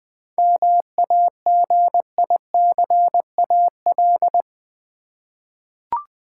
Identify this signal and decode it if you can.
Morse code, heard as a keyed tone, about 20 wpm, 700 Hz. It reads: MAGICAL
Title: 7 Letter Words - First 3 Letters Given - 20wpm